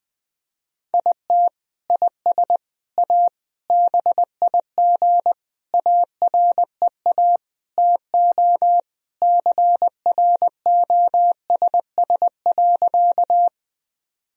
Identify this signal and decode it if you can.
Morse code, heard as a keyed tone, about 20 wpm, 700 Hz. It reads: IT IS A BIG AREA TO CROSS.
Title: Sentences from Top 1000 Words - Review Part 2 - 20wpm